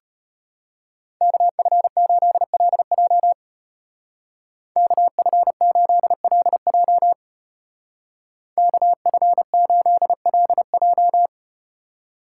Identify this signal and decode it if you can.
Morse code, heard as a keyed tone, about 34 wpm, 700 Hz. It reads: KF8LJ KF8LJ KF8LJ